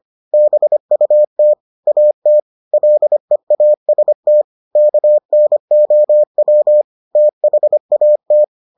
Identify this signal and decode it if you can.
Morse code, heard as a keyed tone, about 25 wpm, 600 Hz. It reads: BUT AT LEAST KNOW THAT